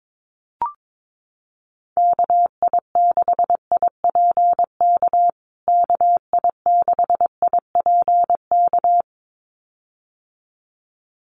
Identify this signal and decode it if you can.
Morse code, heard as a keyed tone, about 22 wpm, 700 Hz. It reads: KI6IPK KI6IPK
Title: USA Call Signs Repeated 2x 22wpm